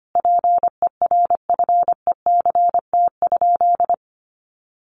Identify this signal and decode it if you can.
Morse code, heard as a keyed tone, about 25 wpm, 700 Hz. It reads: PERFECT?
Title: Sentences from Top 1000 Words 25wpm